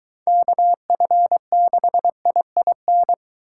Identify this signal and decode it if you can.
Morse code, heard as a keyed tone, about 22 wpm, 700 Hz. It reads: KF6IIN